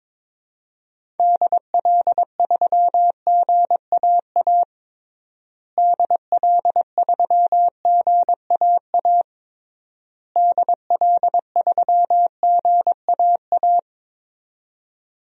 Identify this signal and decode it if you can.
Morse code, heard as a keyed tone, about 22 wpm, 700 Hz. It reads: DL3GAA DL3GAA DL3GAA
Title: DXCC-100 Call Signs - 6 Characters - Repeated 3x - 22wpm